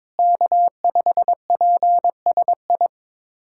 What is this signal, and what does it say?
Morse code, heard as a keyed tone, about 22 wpm, 700 Hz. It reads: K5PSI
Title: USA Callsign DE Callsign - 5 Characters - 22wpm